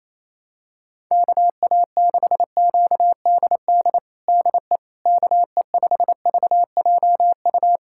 Morse code, heard as a keyed tone, about 28 wpm, 700 Hz. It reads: KA6QDD DE KE5VJU
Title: USA Callsign DE Callsign 28wpm